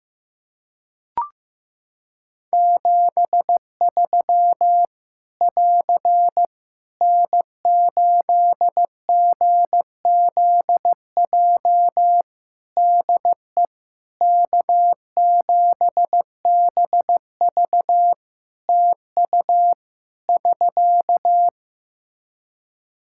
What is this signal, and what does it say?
Morse code, heard as a keyed tone, about 15 wpm, 700 Hz. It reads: 73 <AR> N8GZJ DE K7BV TU <SK>
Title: Practice QSO - Exchange 4 - Ending - 15wpm